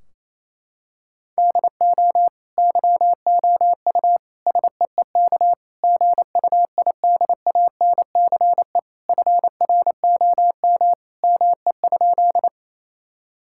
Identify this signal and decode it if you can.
Morse code, heard as a keyed tone, about 28 wpm, 700 Hz. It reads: DO YOU SEEK GUIDANCE FROM ME?
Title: Sentences from Top 1600 Words 28wpm